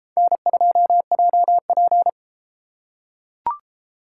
Morse code, heard as a keyed tone, about 33 wpm, 700 Hz. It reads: N2JP